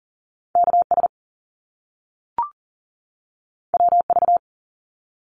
Morse code, heard as a keyed tone, about 41 wpm, 700 Hz. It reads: KS WV
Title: US States 40wpm